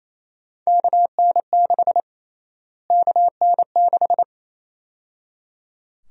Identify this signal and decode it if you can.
Morse code, heard as a keyed tone, about 28 wpm, 700 Hz. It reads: KN6 KN6
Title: USA Call Sign Prefixes - Repeated 2x - 28wpm